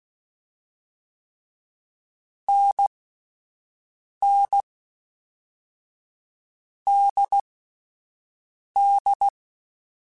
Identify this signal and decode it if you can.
Morse code, heard as a keyed tone, about 16 wpm, 800 Hz. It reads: N N D D
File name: Short Runs / Short Runs-mp3s / NDB NDB6 @10